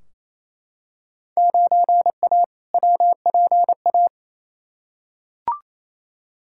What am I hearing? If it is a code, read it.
Morse code, heard as a keyed tone, about 28 wpm, 700 Hz. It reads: 9A WPA
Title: Field Day Exchanges Encore - 28wpm